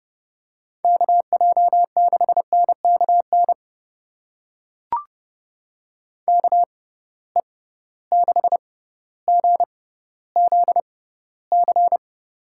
Morse code, heard as a keyed tone, about 30 wpm, spaced as for 15 wpm, 700 Hz. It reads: KJ6NKN KE6GZC
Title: USA Call Signs - 6 Characters - 6x Character Spacing - 30wpm